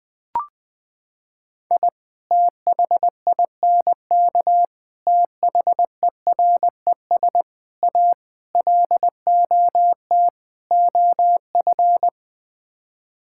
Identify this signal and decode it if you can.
Morse code, heard as a keyed tone, about 20 wpm, 700 Hz. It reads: I THINK THERES A LOT OF
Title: Sets of 6 Words - 20wpm